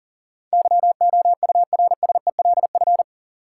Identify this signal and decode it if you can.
Morse code, heard as a keyed tone, about 40 wpm, 700 Hz. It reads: YOURSELF
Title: Sentences from Top 600 Words - New Word Drill - 40wpm